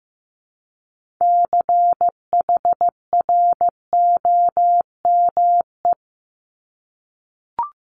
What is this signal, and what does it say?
Morse code, heard as a keyed tone, about 15 wpm, 700 Hz. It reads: CHROME
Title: Multi-letter Phonemes 15wpm